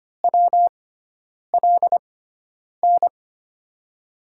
Morse code, heard as a keyed tone, about 25 wpm, spaced as for 9 wpm, 700 Hz. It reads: WLN
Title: USA Call Signs - 6x Character Spacing - 25wpm